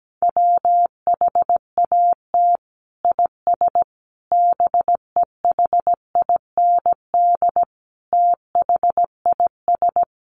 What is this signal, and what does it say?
Morse code, heard as a keyed tone, about 17 wpm, 700 Hz. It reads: WHAT IS BEHIND THIS
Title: Sentences from Top 1100 Words 17wpm